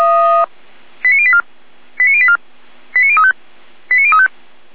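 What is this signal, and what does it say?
Five-tone selective call (ZVEI), heard as quick runs of five tones, tones of about 70 ms. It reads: 89084 89084 89035 89035
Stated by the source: ZVEI five-tone selective call received on an SDR.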